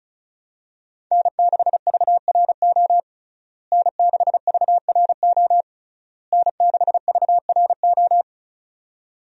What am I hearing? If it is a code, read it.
Morse code, heard as a keyed tone, about 36 wpm, 700 Hz. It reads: N6VRO N6VRO N6VRO
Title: USA Call Signs - 5 Characters - Repeated 3x - 35wpm